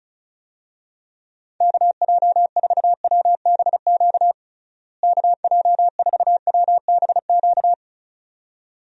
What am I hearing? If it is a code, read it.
Morse code, heard as a keyed tone, about 34 wpm, 700 Hz. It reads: KJ4WBQ KJ4WBQ
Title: USA Call Signs - 6 Characters - Repeated 2x - 35wpm